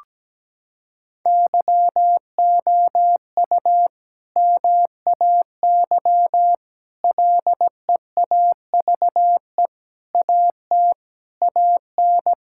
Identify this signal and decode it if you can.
Morse code, heard as a keyed tone, about 17 wpm, 700 Hz. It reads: YOU MAY LEAVE AT AN
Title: Sentences from Top 1000 Words - Review Part 5 - 17wpm